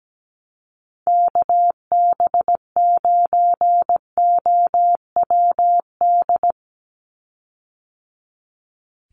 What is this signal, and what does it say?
Morse code, heard as a keyed tone, about 17 wpm, 700 Hz. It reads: KB9OWD
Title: Call Signs - 17wpm